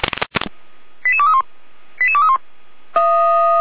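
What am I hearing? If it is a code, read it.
Five-tone selective call (ZVEI), heard as quick runs of five tones, tones of about 70 ms. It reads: 88231 88231
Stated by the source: Five-tone ZVEI selective call received by SDR.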